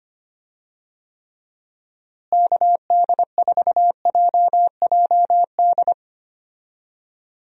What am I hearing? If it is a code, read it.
Morse code, heard as a keyed tone, about 25 wpm, 700 Hz. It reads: KD4JJD
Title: USA Call Signs - 6 Characters - Repeated 2x - 25wpm